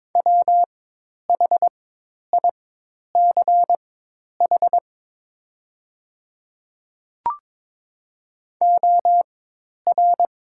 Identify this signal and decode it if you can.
Morse code, heard as a keyed tone, about 22 wpm, spaced as for 10 wpm, 700 Hz. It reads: WHICH OR
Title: 4x Character Spacing - Top 100 Words - 22wpm